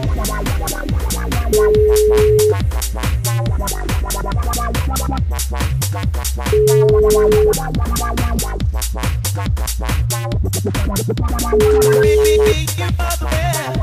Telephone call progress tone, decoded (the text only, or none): ringback tone